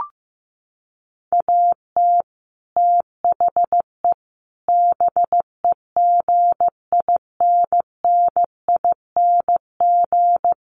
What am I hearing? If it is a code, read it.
Morse code, heard as a keyed tone, about 15 wpm, 700 Hz. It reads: AT THE BEGINNING
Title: Sets of 5 Words 15wpm